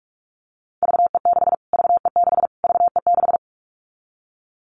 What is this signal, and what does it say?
Morse code, heard as a keyed tone, about 46 wpm, 700 Hz. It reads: VE6 VE6 VE6
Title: Callsign Prefixes for Top 25 DXCC - Repeated 3x - 45wpm